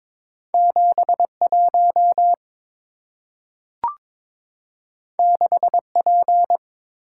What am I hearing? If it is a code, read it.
Morse code, heard as a keyed tone, about 22 wpm, 690 Hz. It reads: 71 6P
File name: practice / 22wpm / Two Letter-Number Combinations - 22wpm